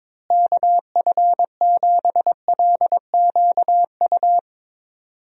Morse code, heard as a keyed tone, about 22 wpm, 700 Hz. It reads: KF7LQU
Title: USA Call Signs - 6 Characters - Repeated 2x - 22wpm